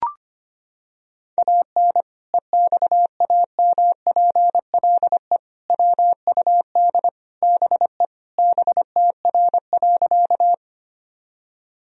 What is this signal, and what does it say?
Morse code, heard as a keyed tone, about 25 wpm, 700 Hz. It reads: AN EXAMPLE WUD BE BTR.
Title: Sentences from Top 400 Words and Abbreviations 25wpm